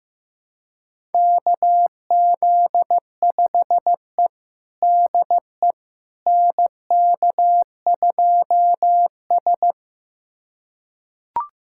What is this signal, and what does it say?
Morse code, heard as a keyed tone, about 16 wpm, 700 Hz. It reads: KZ5E DE NK2S
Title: USA Callsign DE Callsign - 4 Characters - 15wpm